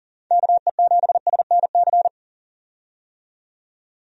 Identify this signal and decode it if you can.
Morse code, heard as a keyed tone, about 40 wpm, 690 Hz. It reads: KE7SNC
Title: USA Call Signs - 6 Characters - 40wpm